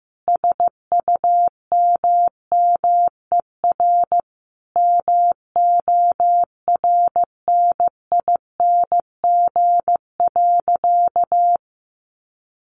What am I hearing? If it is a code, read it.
Morse code, heard as a keyed tone, about 14 wpm, 700 Hz. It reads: SUMMER MORNING.
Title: Sentences from Top 1100 Words 15wpm